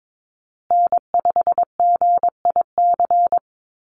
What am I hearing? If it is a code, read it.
Morse code, heard as a keyed tone, about 22 wpm, 700 Hz. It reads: N5GIC
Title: USA Call Signs Repeated 2x 22wpm